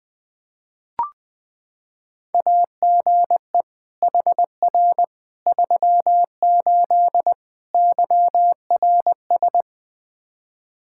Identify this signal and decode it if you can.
Morse code, heard as a keyed tone, about 21 wpm, 700 Hz. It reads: AGE HR 38 YRS
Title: Practice QSO - Exchange 3 - More Info - 20wpm